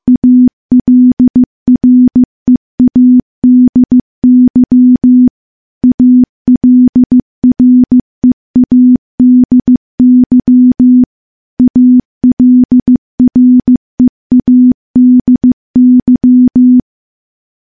Morse code, heard as a keyed tone, about 15 wpm, 250 Hz. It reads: ALREADY ALREADY ALREADY
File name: morse / morse-training-15WPM